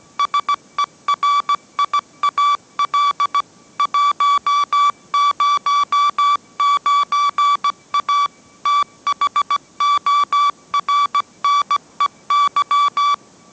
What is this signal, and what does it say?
Morse code, heard as a keyed tone, about 17 wpm, 1150 Hz. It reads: SERIAL 109A THORNEY
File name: mrx / mrx080 18wpm